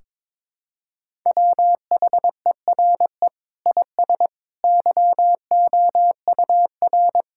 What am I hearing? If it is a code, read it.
Morse code, heard as a keyed tone, about 22 wpm, 700 Hz. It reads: WHERE IS YOUR
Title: Sentences from Top 1600 Words 22wpm